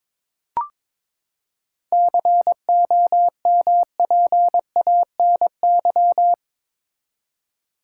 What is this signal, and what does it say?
Morse code, heard as a keyed tone, about 21 wpm, 700 Hz. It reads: COMPANY
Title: Seven Letter Words - 22wpm